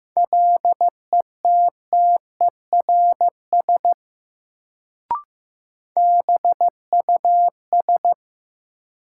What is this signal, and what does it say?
Morse code, heard as a keyed tone, about 15 wpm, 700 Hz. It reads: LETTERS BUS
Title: Sentences from Top 1000 Words - Review Part 6 - New Word Drill - 15wpm